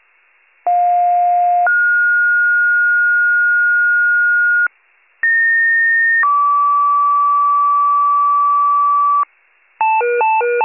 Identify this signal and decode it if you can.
Two-tone page, heard as a one-second tone, then a three-second tone, about 710 then 1430 Hz, about 1850 then 1150 Hz, with noise. It is heard from 0.7 to 9.2 s.